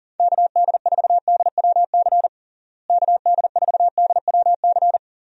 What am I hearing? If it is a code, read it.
Morse code, heard as a keyed tone, about 40 wpm, 700 Hz. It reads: KD4DWC KD4DWC
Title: USA Call Signs - 6 Characters - Repeated 3x - 40wpm